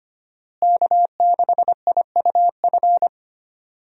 Morse code, heard as a keyed tone, about 25 wpm, 700 Hz. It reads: K6IUF
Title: USA Call Signs - 5 Characters - Repeated 2x - 25wpm